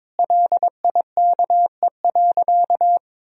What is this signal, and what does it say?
Morse code, heard as a keyed tone, about 22 wpm, 700 Hz. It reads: LIKE.
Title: Sentences from Top 500 Words and Abbreviations 22wpm